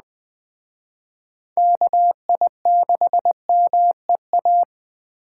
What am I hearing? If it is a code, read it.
Morse code, heard as a keyed tone, about 20 wpm, 700 Hz. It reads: KI6MEA
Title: Speed Racing - USA 2x3 Call Signs 17wpm